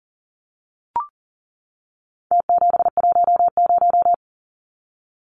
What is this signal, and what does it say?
Morse code, heard as a keyed tone, about 40 wpm, 700 Hz. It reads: T710
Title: Numbers - Contest Style - 4-Digit 40wpm